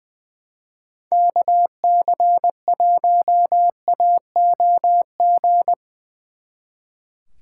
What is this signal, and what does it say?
Morse code, heard as a keyed tone, about 22 wpm, 700 Hz. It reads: KC1AOG
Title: Speed Racing - USA 2x3 Call Signs 20wpm